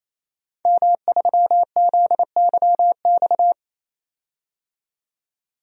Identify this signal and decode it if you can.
Morse code, heard as a keyed tone, about 28 wpm, 700 Hz. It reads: M3ZYX
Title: DXCC-100 Call Signs - 5 Characters - 28wpm